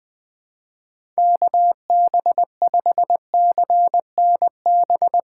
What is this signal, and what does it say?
Morse code, heard as a keyed tone, about 20 wpm, 700 Hz. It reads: KB5CNB
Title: Speed Racing - USA 2x3 Call Signs 17wpm